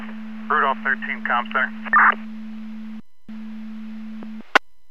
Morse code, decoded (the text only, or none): none